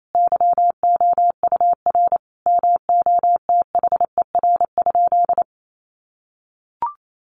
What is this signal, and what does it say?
Morse code, heard as a keyed tone, about 28 wpm, 700 Hz. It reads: YOUR MOTHER?
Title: Sentences from Top 1100 Words 28wpm